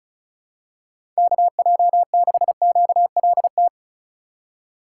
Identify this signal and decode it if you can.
Morse code, heard as a keyed tone, about 34 wpm, 700 Hz. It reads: KJ6QLT
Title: USA Call Signs - 6 Characters - 3x Character Spacing - 35wpm